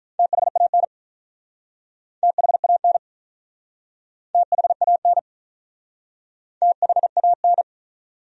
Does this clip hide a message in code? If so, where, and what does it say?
Yes: Morse code from 0.2 to 7.6 s, heard as a keyed tone, about 47 wpm, 700 Hz. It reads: THAN THAN THAN THAN